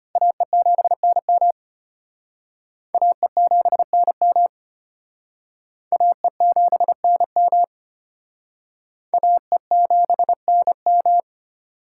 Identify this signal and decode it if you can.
Morse code, heard as a keyed tone, about 32 wpm, 700 Hz. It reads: AE7NM AE7NM AE7NM AE7NM